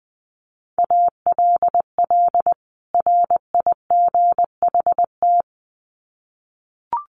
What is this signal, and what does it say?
Morse code, heard as a keyed tone, about 19 wpm, 700 Hz. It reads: ALL RIGHT
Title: Sets of 2 Words 20wpm